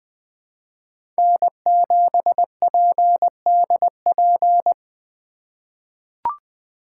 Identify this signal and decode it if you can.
Morse code, heard as a keyed tone, about 20 wpm, 700 Hz. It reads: N7PDP
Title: USA Call Signs - 5 Characters - 4x Character Spacing - 20wpm